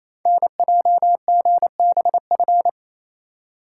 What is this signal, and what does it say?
Morse code, heard as a keyed tone, about 28 wpm, 700 Hz. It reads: NJGBF
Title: Code Groups - 5 Letters - 28wpm